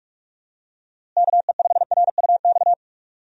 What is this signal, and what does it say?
Morse code, heard as a keyed tone, about 46 wpm, 700 Hz. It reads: KE5RUX
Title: USA Call Signs - 6 Characters - Repeated 2x - 45wpm